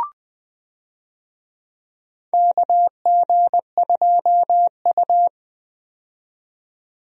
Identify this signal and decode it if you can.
Morse code, heard as a keyed tone, about 20 wpm, 700 Hz. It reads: KG2U
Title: USA Call Signs - 4 Characters - 20wpm